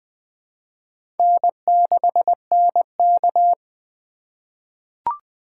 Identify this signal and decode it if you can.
Morse code, heard as a keyed tone, about 20 wpm, 700 Hz. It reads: N6NK
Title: USA Call Signs - 3x Character Spacing - 20wpm